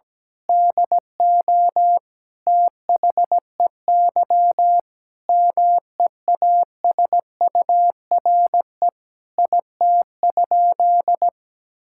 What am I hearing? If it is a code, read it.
Morse code, heard as a keyed tone, about 17 wpm, 700 Hz. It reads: DO THEY MEASURE IT?
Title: Sentences from Top 1000 Words - Review Part 6 - 17wpm